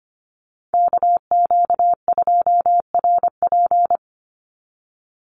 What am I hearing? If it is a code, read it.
Morse code, heard as a keyed tone, about 25 wpm, 700 Hz. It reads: KQ2RP
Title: NA SOTA Chaser Callsigns 25wpm